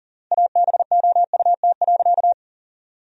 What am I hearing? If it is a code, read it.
Morse code, heard as a keyed tone, about 37 wpm, 700 Hz. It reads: ABOUT.
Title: 3x Word-Spacing - Sentences from Top 400 Words - 40wpm